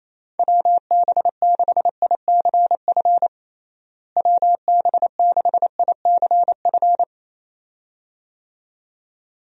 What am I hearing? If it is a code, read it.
Morse code, heard as a keyed tone, about 27 wpm, 700 Hz. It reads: WB6ICF WB6ICF
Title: USA Call Signs - 6 Characters - Repeated 2x - 28wpm